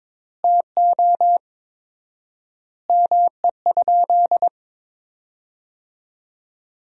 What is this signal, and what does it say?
Morse code, heard as a keyed tone, about 22 wpm, 700 Hz. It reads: TO ME?
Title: 4x Word-Spacing - Sentences from Top 200 Words - 22wpm